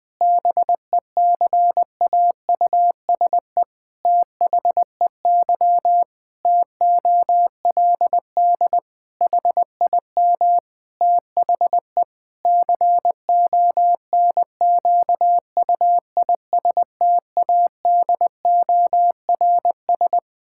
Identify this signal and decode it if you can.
Morse code, heard as a keyed tone, about 20 wpm, 700 Hz. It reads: BECAUSE THEY TOLD HIM THE CONQUISTADORS
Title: Pangram Challenge v2 - 20wpm